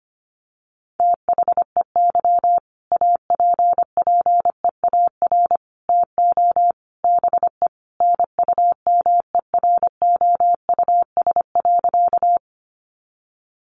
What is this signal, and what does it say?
Morse code, heard as a keyed tone, about 25 wpm, 700 Hz. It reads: THEY APPEAR TO BE NUMEROUS.
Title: Sentences from Top 1400 Words 25wpm